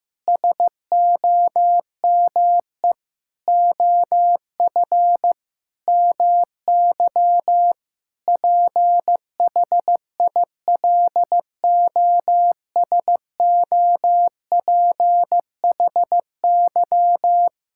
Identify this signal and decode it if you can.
Morse code, heard as a keyed tone, about 15 wpm, 700 Hz. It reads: SOME OF MY PHILOSOPHY
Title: Sentences from Top 1000 Words - Review Part 6 - 15wpm